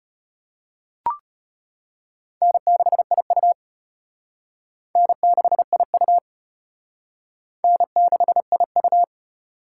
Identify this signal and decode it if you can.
Morse code, heard as a keyed tone, about 31 wpm, 700 Hz. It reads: N6IU N6IU N6IU
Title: Speed Racing - USA 1x2 Call Signs 25wpm